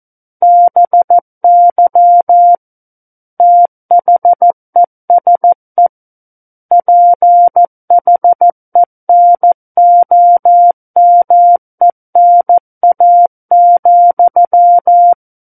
Morse code, heard as a keyed tone, about 14 wpm, 700 Hz. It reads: BY THESE PHENOMENA,
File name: Book One / 05-Chapter 5 - 14 WPM